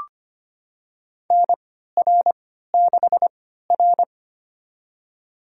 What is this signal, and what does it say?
Morse code, heard as a keyed tone, about 25 wpm, spaced as for 14 wpm, 700 Hz. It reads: NR6R
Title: USA Call Signs - 4 Characters - 3x Character Spacing - 25wpm